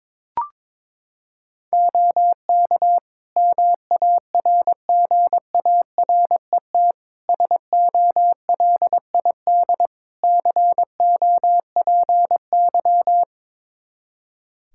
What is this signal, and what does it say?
Morse code, heard as a keyed tone, about 22 wpm, 700 Hz. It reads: OK MARGARET SOLID COPY
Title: Practice QSO - Exchange 3 - More Info - 22wpm